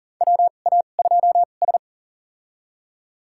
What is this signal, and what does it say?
Morse code, heard as a keyed tone, about 40 wpm, spaced as for 29 wpm, 700 Hz. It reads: WA2S
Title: USA Call Signs - 4 Characters - 2x Character Spacing - 40wpm